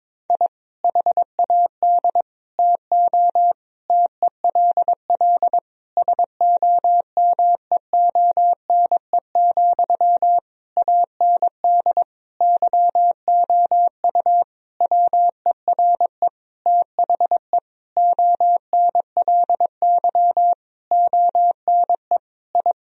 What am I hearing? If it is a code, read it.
Morse code, heard as a keyed tone, about 22 wpm, 700 Hz. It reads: I HAD TO TELL SOMEONE, AND YOU WERE THE ONLY ONE I
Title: Sentences from Top 1000 Words - Review Part 4 - 22wpm